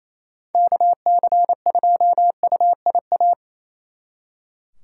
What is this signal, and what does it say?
Morse code, heard as a keyed tone, about 36 wpm, 700 Hz. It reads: KC2UIA